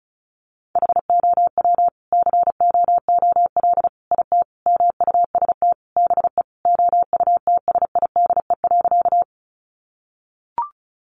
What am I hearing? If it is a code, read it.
Morse code, heard as a keyed tone, about 34 wpm, 700 Hz. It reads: HOW COOL IT MUST BE OUTSIDE.
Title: Sentences from Top 1200 Words 35wpm